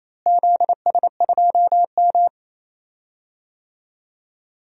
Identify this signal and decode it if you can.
Morse code, heard as a keyed tone, about 27 wpm, 700 Hz. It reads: ZS2M
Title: DXCC-100 Call Signs - 4 Characters - Repeated 3x - 28wpm